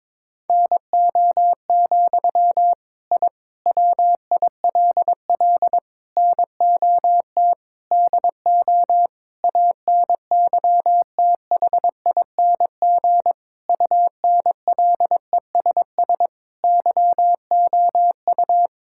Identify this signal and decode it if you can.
Morse code, heard as a keyed tone, about 22 wpm, 700 Hz. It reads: NO, I WILL NOT DO ANYTHING UNLESS YOU
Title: Sentences from Top 1000 Words - Review Part 6 - 22wpm